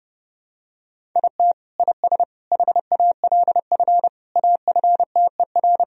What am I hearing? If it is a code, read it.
Morse code, heard as a keyed tone, about 32 wpm, 700 Hz. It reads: IT IS HALF AFTER
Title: Sentences from Top 1000 Words - Review Part 1 - 30wpm